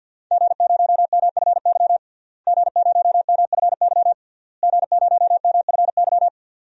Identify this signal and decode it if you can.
Morse code, heard as a keyed tone, about 52 wpm, 700 Hz. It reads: G0MFY G0MFY G0MFY
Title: DXCC-100 Call Signs - 5 Characters - Repeated 3x - 50wpm